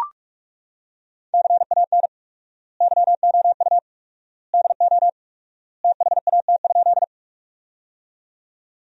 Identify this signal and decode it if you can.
Morse code, heard as a keyed tone, about 42 wpm, 700 Hz. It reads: CAN YOU DO THAT?
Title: 4x Word-Spacing - Sentences from Top 100 Words - 45wpm